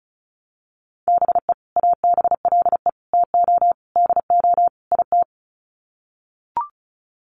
Morse code, heard as a keyed tone, about 34 wpm, 700 Hz. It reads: BE ABLE TO DO IT
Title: Sets of 5 Words Encore2 35wpm